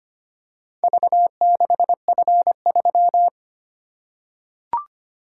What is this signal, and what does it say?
Morse code, heard as a keyed tone, about 25 wpm, 700 Hz. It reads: V6F3
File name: practice / 25wpm / Four Letter-Number Combinations - 25wpm